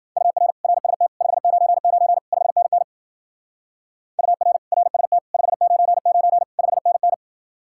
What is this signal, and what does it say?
Morse code, heard as a keyed tone, about 49 wpm, 700 Hz. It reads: UR RST 599 5NN UR RST 599 5NN
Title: Speed-Racing Common QSO Phrases - 40wpm